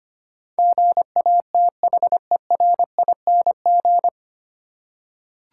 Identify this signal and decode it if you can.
Morse code, heard as a keyed tone, about 25 wpm, 700 Hz. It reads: GATHERING
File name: Nine Letter Words - 25wpm